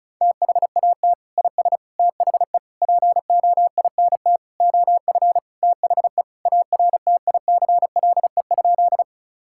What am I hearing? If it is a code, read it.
Morse code, heard as a keyed tone, about 36 wpm, 700 Hz. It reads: THAT IS THE POINT OF THE ARTICLE?
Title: Sentences from Top 1000 Words - Review Part 5 - 35wpm